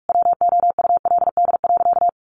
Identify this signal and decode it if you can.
Morse code, heard as a keyed tone, about 44 wpm, 700 Hz. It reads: WOULD.
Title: Sentences from Top 1500 Words 45wpm